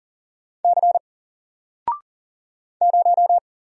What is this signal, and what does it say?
Morse code, heard as a keyed tone, about 40 wpm, 700 Hz. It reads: C 0
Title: Single Character (Letter:Number:Punctuation:Prosign) Alt 40wpm